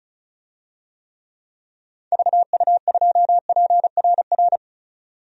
Morse code, heard as a keyed tone, about 34 wpm, 700 Hz. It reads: VU2PRR
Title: DXCC-100 Call Signs - 6 Characters - 35wpm